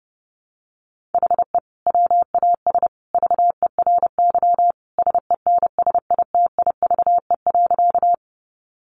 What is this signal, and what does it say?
Morse code, heard as a keyed tone, about 30 wpm, 700 Hz. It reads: HE WAS VERY SENSITIVE.
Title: Sentences from Top 1400 Words 30wpm